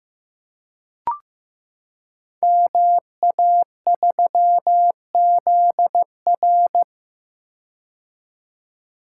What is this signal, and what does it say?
Morse code, heard as a keyed tone, about 15 wpm, 700 Hz. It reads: MA3ZR
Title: Code Groups - 5 Letter:Numbers - 15wpm